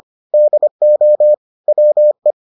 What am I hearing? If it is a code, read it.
Morse code, heard as a keyed tone, about 25 wpm, 600 Hz. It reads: DO WE